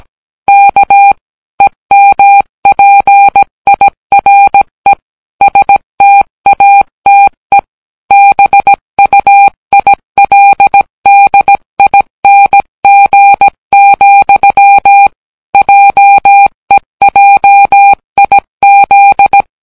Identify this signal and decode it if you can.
Morse code, heard as a keyed tone, about 17 wpm, 800 Hz. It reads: K EMPIRE STATE BUILDING, JEJIZ